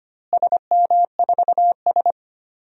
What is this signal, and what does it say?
Morse code, heard as a keyed tone, about 25 wpm, 700 Hz. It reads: SM4S
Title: DXCC-100 Call Signs - 4 Characters - Repeated 3x - 25wpm